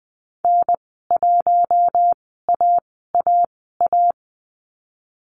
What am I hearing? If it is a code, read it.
Morse code, heard as a keyed tone, about 20 wpm, spaced as for 14 wpm, 700 Hz. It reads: N1AAA
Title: 042 Lesson 8g - Number 1 - Call Signs 20wpm